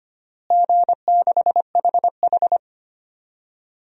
Morse code, heard as a keyed tone, about 25 wpm, 700 Hz. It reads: G6HH
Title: DXCC-100 Call Signs - 4 Characters - Repeated 3x - 25wpm